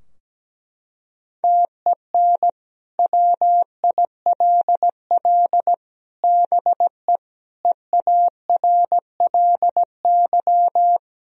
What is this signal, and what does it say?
Morse code, heard as a keyed tone, about 17 wpm, 700 Hz. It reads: TEN WILL BE EARLY
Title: Sentences from Top 1000 Words - Review Part 2 - 17wpm